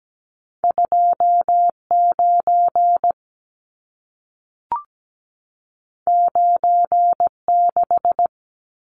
Morse code, heard as a keyed tone, about 17 wpm, 700 Hz. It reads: 29 96
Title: Numbers - 2-Digit 17wpm